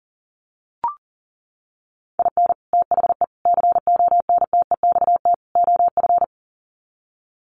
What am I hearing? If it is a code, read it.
Morse code, heard as a keyed tone, about 40 wpm, 700 Hz. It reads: IN THE CONTEXT OF
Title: Sets of 4 Words 40wpm